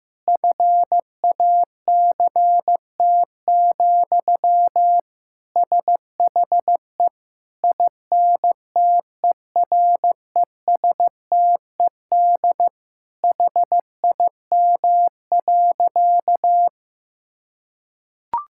Morse code, heard as a keyed tone, about 15 wpm, 700 Hz. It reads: FACT, SHE INTERESTED HIM.
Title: Sentences from Top 1000 Words - Review Part 6 - 15wpm